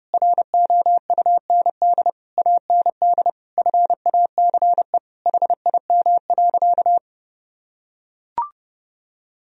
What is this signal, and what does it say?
Morse code, heard as a keyed tone, about 30 wpm, 700 Hz. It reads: ROUND AND FACE HIM.
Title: Sentences from Top 1000 Words - Review Part 2 - 30wpm